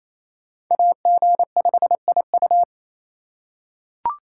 Morse code, heard as a keyed tone, about 28 wpm, 700 Hz. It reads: AG5IU
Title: USA Call Signs - 4x Character Spacing - 28wpm